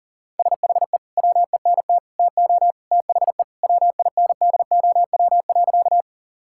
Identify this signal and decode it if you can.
Morse code, heard as a keyed tone, about 37 wpm, 700 Hz. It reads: SHE WENT TO THE WINDOW.
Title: Sentences from Top 1600 Words 40wpm